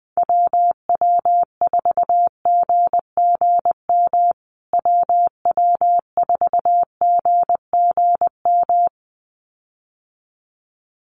Morse code, heard as a keyed tone, about 20 wpm, 700 Hz. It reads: WW4GGM WW4GGM
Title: USA Call Signs Repeated 2x 20wpm